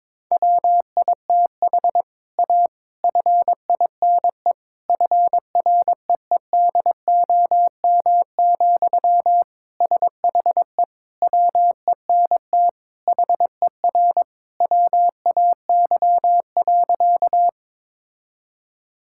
Morse code, heard as a keyed tone, about 22 wpm, 700 Hz. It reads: WITH A FINE FREEDOM, SHE WENT HER WAY.
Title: Sentences from Top 1000 Words - Review Part 5 - 22wpm